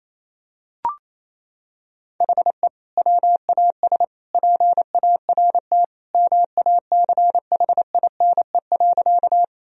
Morse code, heard as a keyed tone, about 27 wpm, 700 Hz. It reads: HE WAS PART MACHINE.
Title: Sentences from Top 1000 Words - Review Part 2 - 28wpm